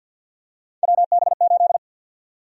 Morse code, heard as a keyed tone, about 51 wpm, 700 Hz. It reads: WB8
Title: USA Call Sign Prefixes - 50wpm